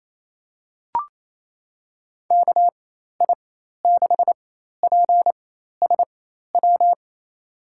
Morse code, heard as a keyed tone, about 27 wpm, spaced as for 13 wpm, 700 Hz. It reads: KI6PSW
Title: USA Call Signs - 6 Characters - 4x Character Spacing - 28wpm